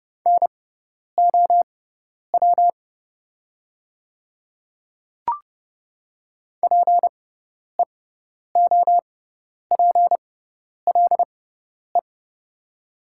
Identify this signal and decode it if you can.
Morse code, heard as a keyed tone, about 31 wpm, spaced as for 10 wpm, 700 Hz. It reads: NOW PEOPLE
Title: 6x Character Spacing - Top 300 Words - 30wpm